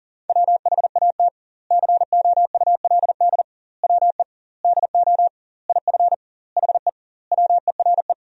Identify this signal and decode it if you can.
Morse code, heard as a keyed tone, about 40 wpm, 700 Hz. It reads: WHAT COULD WE DO IF HE WERE
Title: 2x Word-Spacing - Sentences from Top 400 Words - 40wpm